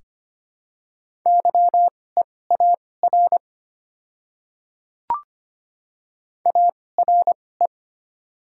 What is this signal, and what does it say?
Morse code, heard as a keyed tone, about 25 wpm, spaced as for 18 wpm, 700 Hz. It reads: YEAR ARE
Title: 2x Character Spacing - Top 300 Words - 25wpm